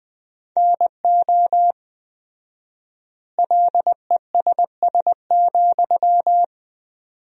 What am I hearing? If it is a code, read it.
Morse code, heard as a keyed tone, about 20 wpm, 700 Hz. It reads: NO LESS,
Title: 4x Word-Spacing - Sentences from Top 400 Words - 20wpm